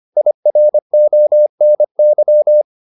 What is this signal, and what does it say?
Morse code, heard as a keyed tone, about 25 wpm, 600 Hz. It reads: IRONY